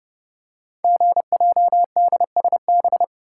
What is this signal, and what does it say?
Morse code, heard as a keyed tone, about 28 wpm, 700 Hz. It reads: GJDSB
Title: Code Groups - 5 Letters - 30wpm